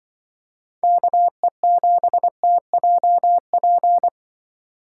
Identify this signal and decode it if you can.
Morse code, heard as a keyed tone, about 25 wpm, 700 Hz. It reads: KE7TJP
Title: Speed Racing - USA Call Signs 20wpm